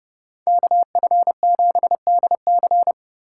Morse code, heard as a keyed tone, about 34 wpm, 700 Hz. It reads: KF7DC